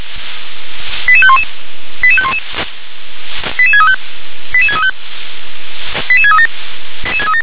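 Five-tone selective call (ZVEI), heard as quick runs of five tones, tones of about 70 ms. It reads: 88511 88511 88635 88635 88637
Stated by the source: ZVEI five-tone selective call 88510 received by SDR.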